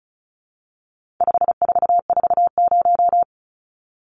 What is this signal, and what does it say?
Morse code, heard as a keyed tone, about 34 wpm, 700 Hz. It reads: 5440
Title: Numbers - 4-Digit 35wpm